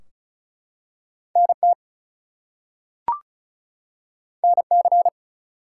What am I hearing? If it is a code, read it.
Morse code, heard as a keyed tone, about 36 wpm, 700 Hz. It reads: NT NC
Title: US States and Canadian Provinces 35wpm